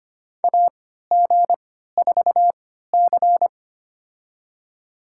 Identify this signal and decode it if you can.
Morse code, heard as a keyed tone, about 25 wpm, spaced as for 14 wpm, 700 Hz. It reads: AG4C
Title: USA Call Signs - 4 Characters - 3x Character Spacing - 25wpm